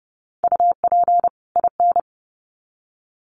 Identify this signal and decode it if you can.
Morse code, heard as a keyed tone, about 30 wpm, 700 Hz. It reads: UP IN
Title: Sets of 2 Words 30wpm